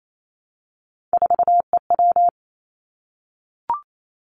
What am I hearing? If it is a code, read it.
Morse code, heard as a keyed tone, about 28 wpm, 700 Hz. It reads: 4EW
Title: Three Challenging Letter-Number Combinations 28wpm